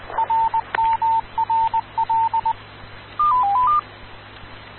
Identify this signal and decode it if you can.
Morse code, heard as a keyed tone, about 20 wpm, 900 Hz. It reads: RMRL
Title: Rocky Mountain Radio League Repeaters